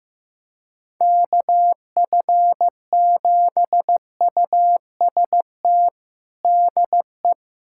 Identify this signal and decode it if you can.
Morse code, heard as a keyed tone, about 16 wpm, 700 Hz. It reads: KF7UST DE
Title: USA Callsign DE Callsign 15wpm